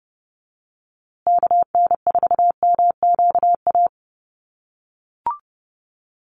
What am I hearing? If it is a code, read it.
Morse code, heard as a keyed tone, about 30 wpm, 700 Hz. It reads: KN4MQA
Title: USA Call Signs Repeated 3x 30wpm